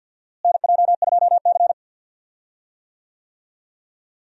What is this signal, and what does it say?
Morse code, heard as a keyed tone, about 51 wpm, 700 Hz. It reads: NJ2C
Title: USA Call Signs - 4 Characters - Repeated 3x - 50wpm